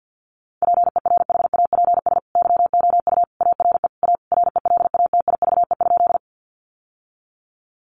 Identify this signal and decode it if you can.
Morse code, heard as a keyed tone, about 50 wpm, 700 Hz. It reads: PERHAPS YOU ARE A RELATIVE?
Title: Sentences from Top 1400 Words 50wpm